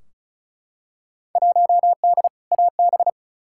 Morse code, heard as a keyed tone, about 36 wpm, 700 Hz. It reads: 1D AB
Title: Field Day Exchanges Encore - 35wpm